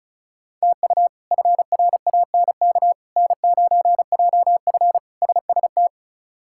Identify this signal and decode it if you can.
Morse code, heard as a keyed tone, about 34 wpm, 700 Hz. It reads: TU FRANK N9JF SST
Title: K1USN SST QSO Practice - 35wpm